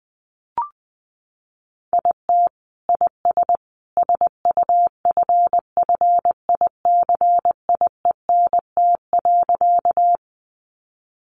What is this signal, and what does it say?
Morse code, heard as a keyed tone, about 20 wpm, 700 Hz. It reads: IT IS SUFFICIENT.
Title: Sentences from Top 1400 Words 20wpm